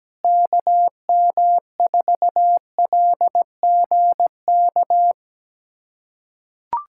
Morse code, heard as a keyed tone, about 17 wpm, 700 Hz. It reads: KM4LGK
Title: USA Callsign DE Callsign - 6 Characters - 17wpm